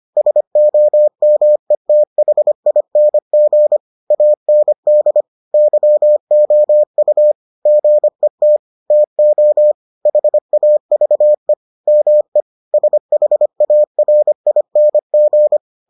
Morse code, heard as a keyed tone, about 25 wpm, 600 Hz. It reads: SOMETHING AND YOU GET TO HAVE ME SHARING